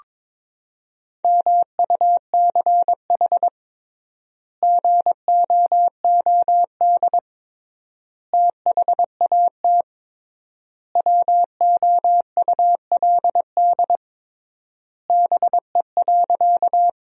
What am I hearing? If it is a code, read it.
Morse code, heard as a keyed tone, about 22 wpm, 700 Hz. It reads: MUCH GOOD THAT WOULD BE.
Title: 3x Word-Spacing - Sentences from Top 200 Words - 22wpm